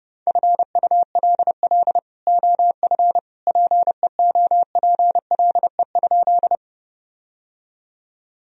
Morse code, heard as a keyed tone, about 30 wpm, 700 Hz. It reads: FULL OF PEOPLE?
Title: Sentences from Top 1000 Words - Review Part 3 - 30wpm